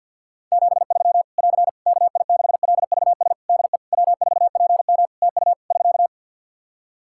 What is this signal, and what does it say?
Morse code, heard as a keyed tone, about 49 wpm, 700 Hz. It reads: 73 <AR> KI6LVS DE W4PM TU <SK>
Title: Practice QSO - Exchange 4 - Ending - 50wpm